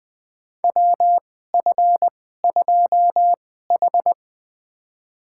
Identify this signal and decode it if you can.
Morse code, heard as a keyed tone, about 20 wpm, spaced as for 15 wpm, 700 Hz. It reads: WF2H
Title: USA Call Signs - 4 Characters - 2x Character Spacing - 20wpm